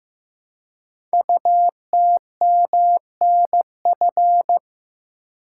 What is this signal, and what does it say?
Morse code, heard as a keyed tone, about 15 wpm, 700 Hz. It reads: UTMNF
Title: Code Groups - 5 Letters - 15wpm